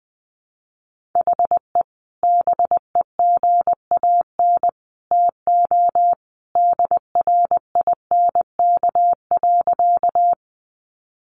Morse code, heard as a keyed tone, about 20 wpm, 700 Hz. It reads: HE BEGAN TO DRINK.
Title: Sentences from Top 1500 Words 20wpm